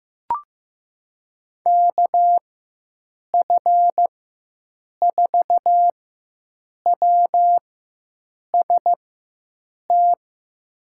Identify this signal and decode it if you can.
Morse code, heard as a keyed tone, about 15 wpm, spaced as for 7 wpm, 700 Hz. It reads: KF4WST
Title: USA Call Signs - 6 Characters - 4x Character Spacing - 15wpm